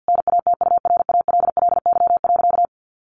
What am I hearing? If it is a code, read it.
Morse code, heard as a keyed tone, about 50 wpm, 700 Hz. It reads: NATURALLY.
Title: Sentences from Top 1500 Words 50wpm